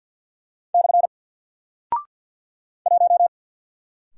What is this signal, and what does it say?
Morse code, heard as a keyed tone, about 48 wpm, 700 Hz. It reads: / 1
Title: Single Character (Letter:Number:Punctuation:Prosign) Alt 50wpm